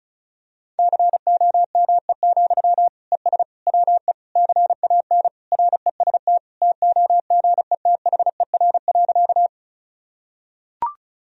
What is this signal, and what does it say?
Morse code, heard as a keyed tone, about 36 wpm, 700 Hz. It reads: COME, ES WE CAN REST TOGETHER.
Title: Sentences from Top 500 Words and Abbreviations 35wpm